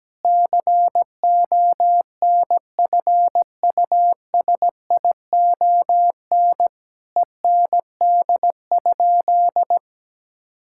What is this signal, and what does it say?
Morse code, heard as a keyed tone, about 17 wpm, 700 Hz. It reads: CONFUSION END?
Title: Sentences from Top 1600 Words 17wpm